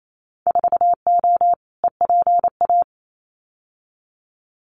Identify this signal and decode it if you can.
Morse code, heard as a keyed tone, about 28 wpm, 700 Hz. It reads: 4O EPA
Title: Winter Field Day Exchanges 28wpm